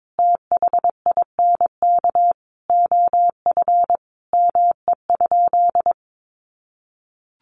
Morse code, heard as a keyed tone, about 22 wpm, 700 Hz. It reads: THINK OF ME?
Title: Sentences from Top 1100 Words 22wpm